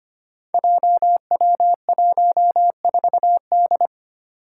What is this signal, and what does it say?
Morse code, heard as a keyed tone, about 25 wpm, 700 Hz. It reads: JW14D
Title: Five Challenging Letter-Number Combinations 25wpm